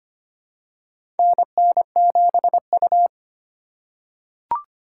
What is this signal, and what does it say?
Morse code, heard as a keyed tone, about 25 wpm, 700 Hz. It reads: NN7U
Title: USA Call Signs - 4 Characters - 3x Character Spacing - 25wpm